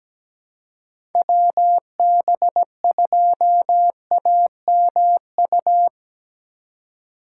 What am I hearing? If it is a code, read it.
Morse code, heard as a keyed tone, about 17 wpm, 700 Hz. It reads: WB2AMU
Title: USA Call Signs - 6 Characters - 2x Character Spacing - 17wpm